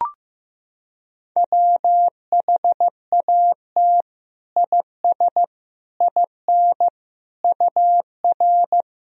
Morse code, heard as a keyed tone, about 15 wpm, 700 Hz. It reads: WHAT IS IN UR
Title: Sentences from Top 200 Words and Abbreviations 15wpm